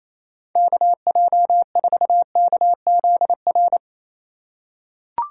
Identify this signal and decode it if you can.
Morse code, heard as a keyed tone, about 28 wpm, 700 Hz. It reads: KJ4KZR
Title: USA Call Signs - 6 Characters - 3x Character Spacing - 28wpm